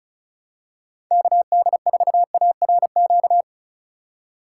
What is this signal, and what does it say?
Morse code, heard as a keyed tone, about 36 wpm, 700 Hz. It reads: KD4ARQ
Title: USA Call Signs - 6 Characters - 3x Character Spacing - 35wpm